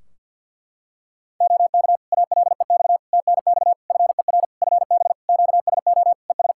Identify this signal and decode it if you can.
Morse code, heard as a keyed tone, about 53 wpm, 700 Hz. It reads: OK ALEX TNX FER FB QSO ES
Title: Practice QSO - Exchange 4 - Ending - 50wpm